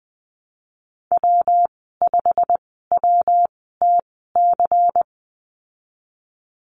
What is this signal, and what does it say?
Morse code, heard as a keyed tone, about 20 wpm, spaced as for 15 wpm, 700 Hz. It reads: W5WTC
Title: 173 Lesson 28g - Number 9 - Call Signs 20wpm